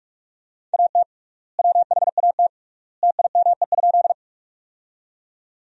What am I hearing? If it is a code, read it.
Morse code, heard as a keyed tone, about 46 wpm, 700 Hz. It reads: AT WHAT TIME?
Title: 3x Word-Spacing - Sentences from Top 100 Words - 45wpm